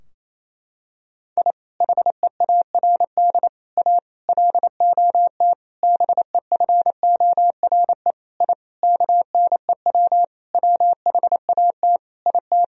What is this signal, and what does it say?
Morse code, heard as a keyed tone, about 28 wpm, 700 Hz. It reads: I HEARD A LOT BEFORE I KNEW WHAT IT
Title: Sentences from Top 1000 Words - Review Part 3 - 28wpm